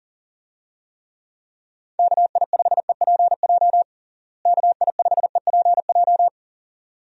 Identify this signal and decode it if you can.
Morse code, heard as a keyed tone, about 40 wpm, 700 Hz. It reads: KI5EPJ KI5EPJ
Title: USA Call Signs - 6 Characters - Repeated 2x - 40wpm